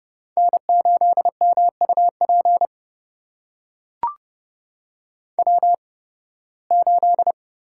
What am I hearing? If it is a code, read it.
Morse code, heard as a keyed tone, about 31 wpm, spaced as for 20 wpm, 700 Hz. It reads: N8MUP W8
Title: USA Call Signs - 5 Characters - 8x Character Spacing - 30wpm